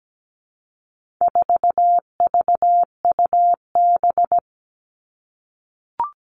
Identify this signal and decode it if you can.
Morse code, heard as a keyed tone, about 17 wpm, 700 Hz. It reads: 4VUB
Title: All Cut Numbers - 4-Digit 17wpm